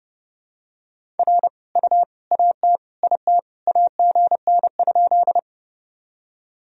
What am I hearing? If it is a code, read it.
Morse code, heard as a keyed tone, about 30 wpm, 700 Hz. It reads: R U AT IT AGN?
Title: Sentences from Top 200 Words and Abbreviations 30wpm